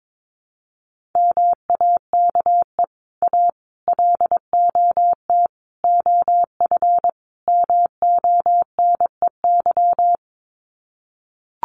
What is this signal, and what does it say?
Morse code, heard as a keyed tone, about 22 wpm, 700 Hz. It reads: MAKE A LOT OF MONEY
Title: Sets of 5 Words Encore2 22wpm